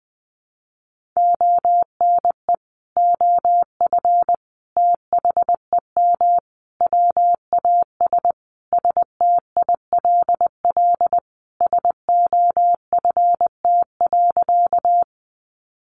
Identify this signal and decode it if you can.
Morse code, heard as a keyed tone, about 20 wpm, 700 Hz. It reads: ONE OF THEM WAS STILL SOFT.
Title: Sentences from Top 1500 Words 20wpm